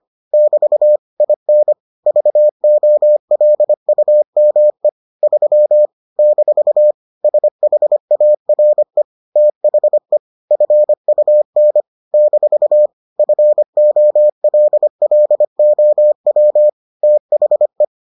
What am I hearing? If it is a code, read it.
Morse code, heard as a keyed tone, about 25 wpm, 600 Hz. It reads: <BT> IN VOLUME 3 - SHARE THE FUN - FOLLOW THE